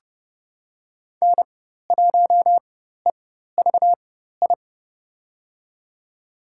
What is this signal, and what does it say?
Morse code, heard as a keyed tone, about 30 wpm, spaced as for 14 wpm, 700 Hz. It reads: N1EVI
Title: USA Call Signs - 5 Characters - 4x Character Spacing - 30wpm